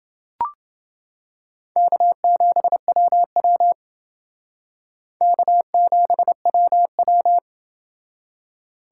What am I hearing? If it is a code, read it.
Morse code, heard as a keyed tone, about 28 wpm, 700 Hz. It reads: K7WW K7WW